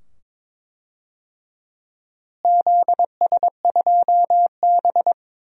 Morse code, heard as a keyed tone, about 22 wpm, 700 Hz. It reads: ZS2B
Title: DXCC-100 Call Signs - 4 Characters - Repeated 2x - 22wpm